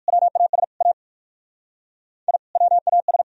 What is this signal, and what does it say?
Morse code, heard as a keyed tone, about 57 wpm, 700 Hz. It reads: WAS A I WAS